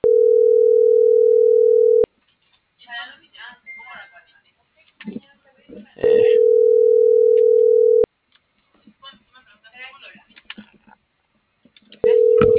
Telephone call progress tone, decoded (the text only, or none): ringback tone